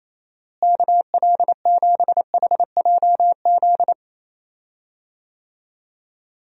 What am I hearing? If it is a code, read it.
Morse code, heard as a keyed tone, about 28 wpm, 700 Hz. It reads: KL7HJZ
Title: DXCC-100 Call Signs - 6 Characters - Repeated 2x - 28wpm